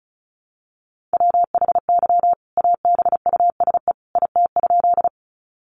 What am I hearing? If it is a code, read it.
Morse code, heard as a keyed tone, about 34 wpm, 700 Hz. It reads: WHY ABUSE IT?
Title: Sentences from Top 1300 Words 35wpm